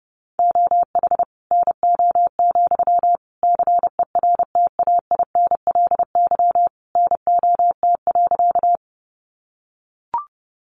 Morse code, heard as a keyed tone, about 30 wpm, 700 Hz. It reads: OH NO, CERTAINLY NOT.
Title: Sentences from Top 900 Words 30wpm